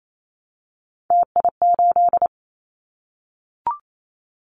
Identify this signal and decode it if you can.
Morse code, heard as a keyed tone, about 28 wpm, 700 Hz. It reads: TI8
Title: Callsign Prefixes for Top 100 DXCC - 28wpm